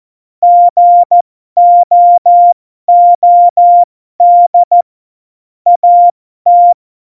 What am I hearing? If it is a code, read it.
Morse code, heard as a keyed tone, about 14 wpm, spaced as for 12 wpm, 700 Hz. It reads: GOOD AT
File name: Morsekurs / Witze